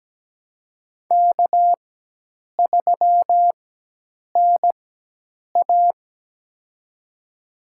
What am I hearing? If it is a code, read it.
Morse code, heard as a keyed tone, about 17 wpm, spaced as for 8 wpm, 700 Hz. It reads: K3NA
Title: USA Call Signs - 4 Characters - 4x Character Spacing - 17wpm